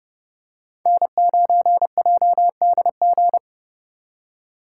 Morse code, heard as a keyed tone, about 32 wpm, 700 Hz. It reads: N9JDG
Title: USA Call Signs - 5 Characters - 6x Character Spacing - 30wpm